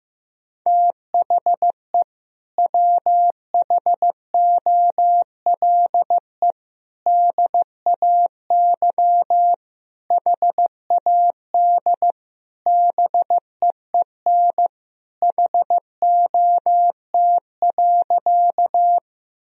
Morse code, heard as a keyed tone, about 15 wpm, 700 Hz. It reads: THE WHOLE DAY HAD BEEN HOT.
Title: Sentences from Top 1000 Words - Review Part 3 - 15wpm